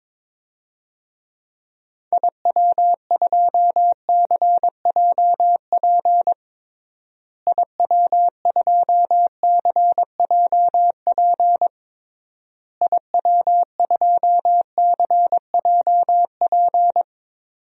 Morse code, heard as a keyed tone, about 22 wpm, 700 Hz. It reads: IW2CJP IW2CJP IW2CJP
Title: DXCC-100 Call Signs - 6 Characters - Repeated 3x - 22wpm